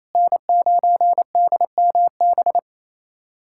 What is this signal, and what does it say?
Morse code, heard as a keyed tone, about 28 wpm, 700 Hz. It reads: N9DMB
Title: USA Callsign DE Callsign - 5 Characters - 28wpm